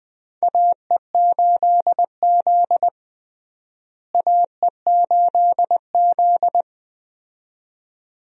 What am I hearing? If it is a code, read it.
Morse code, heard as a keyed tone, about 21 wpm, 700 Hz. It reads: AE8Z AE8Z
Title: USA Call Signs - 4 Characters - Repeated 2x - 20wpm